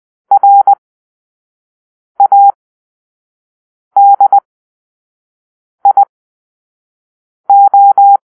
Morse code, heard as a keyed tone, about 20 wpm, spaced as for 5 wpm, 800 Hz. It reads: RADIO
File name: arrl / Grp-2 Drill-3